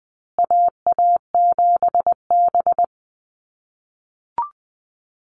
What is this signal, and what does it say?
Morse code, heard as a keyed tone, about 20 wpm, 700 Hz. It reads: AA7B
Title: USA Call Signs Repeated 2x 20wpm